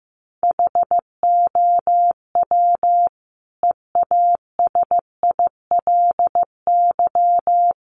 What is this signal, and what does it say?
Morse code, heard as a keyed tone, about 15 wpm, 700 Hz. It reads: HOW EASILY
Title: Sentences from Top 900 Words 15wpm